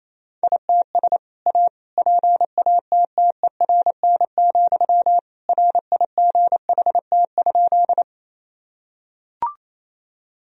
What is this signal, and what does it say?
Morse code, heard as a keyed tone, about 27 wpm, 700 Hz. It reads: ITS A PATTERN, RIGHT?
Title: Sentences from Top 1000 Words - Review Part 2 - 28wpm